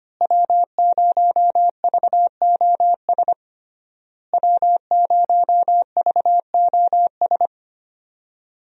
Morse code, heard as a keyed tone, about 25 wpm, 700 Hz. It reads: W0VOS W0VOS
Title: USA Call Signs - 5 Characters - Repeated 3x - 25wpm